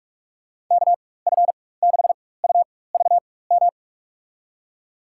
Morse code, heard as a keyed tone, about 46 wpm, spaced as for 21 wpm, 690 Hz. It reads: KF6UVM
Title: USA Call Signs - 4x Character Spacing - 45wpm